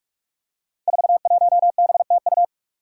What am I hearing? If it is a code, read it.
Morse code, heard as a keyed tone, about 45 wpm, 700 Hz. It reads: 41BTU
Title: Five Challenging Letter-Number Combinations 45wpm